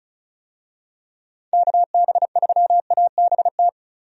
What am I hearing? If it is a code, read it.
Morse code, heard as a keyed tone, about 36 wpm, 700 Hz. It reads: KB3ABT
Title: USA Call Signs - 6 Characters - 35wpm